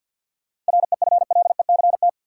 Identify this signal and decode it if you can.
Morse code, heard as a keyed tone, about 53 wpm, 700 Hz. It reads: REFLECT
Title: Sentences from Top 1500 Words - New Word Drill - 50wpm